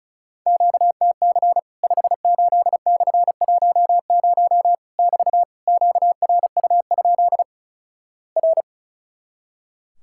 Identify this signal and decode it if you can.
Morse code, heard as a keyed tone, about 35 wpm, 700 Hz. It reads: QTC 58/10 <BT> QRU? R
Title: Worked All Europe DX Contest - QTC Exchange - 35wpm